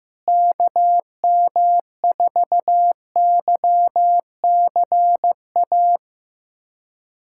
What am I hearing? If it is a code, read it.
Morse code, heard as a keyed tone, about 15 wpm, 700 Hz. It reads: KM4YCA
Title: USA Callsign DE Callsign - 6 Characters - 15wpm